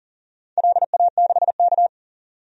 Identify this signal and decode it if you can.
Morse code, heard as a keyed tone, about 40 wpm, 700 Hz. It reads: LA6K
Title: DXCC-100 Call Signs - 4 Characters - 40wpm